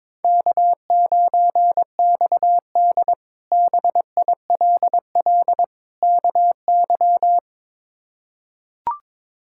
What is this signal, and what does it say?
Morse code, heard as a keyed tone, about 22 wpm, 700 Hz. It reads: K9XD BILL KY
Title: K1USN SST QSO Practice - 22wpm